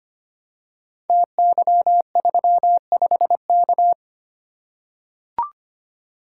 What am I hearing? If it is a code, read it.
Morse code, heard as a keyed tone, about 25 wpm, 700 Hz. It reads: TY35K
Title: Code Groups - 5 Letter:Numbers - 25wpm